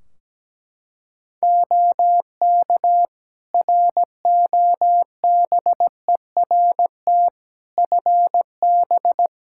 Morse code, heard as a keyed tone, about 17 wpm, 700 Hz. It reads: OK ROBERT FB
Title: K1USN SST QSO Practice 17wpm